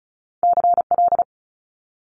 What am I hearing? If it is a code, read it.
Morse code, heard as a keyed tone, about 36 wpm, 700 Hz. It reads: CL
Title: Multi-letter Phonemes Quick Fire 35wpm